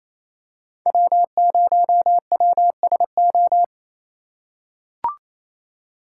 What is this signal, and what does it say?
Morse code, heard as a keyed tone, about 28 wpm, 700 Hz. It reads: W0WSO
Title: USA Call Signs - 5 Characters - 2x Character Spacing - 28wpm